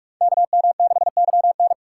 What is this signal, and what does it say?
Morse code, heard as a keyed tone, about 46 wpm, 700 Hz. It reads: KM6YN